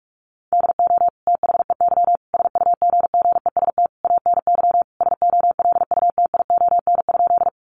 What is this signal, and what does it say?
Morse code, heard as a keyed tone, about 44 wpm, 700 Hz. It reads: DO THEY SUGGEST ANY SOLUTION?
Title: Sentences from Top 1300 Words 45wpm